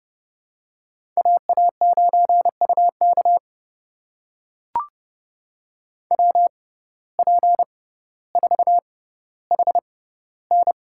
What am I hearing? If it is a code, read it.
Morse code, heard as a keyed tone, about 29 wpm, spaced as for 15 wpm, 700 Hz. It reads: AA9UK WP4HN
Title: USA Call Signs - 5 Characters - 6x Character Spacing - 30wpm